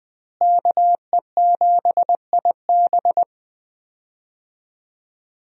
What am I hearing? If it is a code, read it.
Morse code, heard as a keyed tone, about 20 wpm, 700 Hz. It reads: KE7IB
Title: USA Call Signs - 5 Characters - Repeated 3x - 20wpm